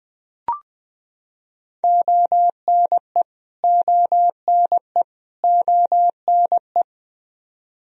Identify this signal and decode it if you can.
Morse code, heard as a keyed tone, about 20 wpm, 700 Hz. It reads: ONE ONE ONE
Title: Top 100 Words Repeated 20wpm